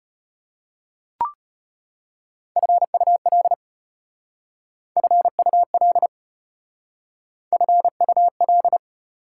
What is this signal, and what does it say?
Morse code, heard as a keyed tone, about 32 wpm, 690 Hz. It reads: FUL FUL FUL